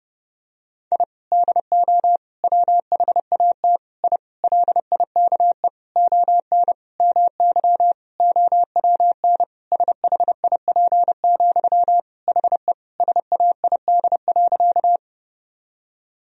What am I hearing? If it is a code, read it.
Morse code, heard as a keyed tone, about 30 wpm, 700 Hz. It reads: I DO WHAT I LIKE ON MY OWN SHIP, HE SAID.
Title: Sentences from Top 1000 Words - Review Part 2 - 30wpm